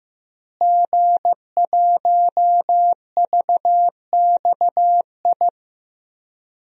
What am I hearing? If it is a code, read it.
Morse code, heard as a keyed tone, about 15 wpm, 700 Hz. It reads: G1VXI
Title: DXCC-100 Call Signs - 5 Characters - Repeated 3x - 15wpm